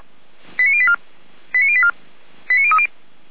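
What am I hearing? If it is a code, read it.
Five-tone selective call (ZVEI), heard as quick runs of five tones, tones of about 70 ms. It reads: 89084 89084 89030
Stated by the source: ZVEI five-tone call 89057 received on SDR.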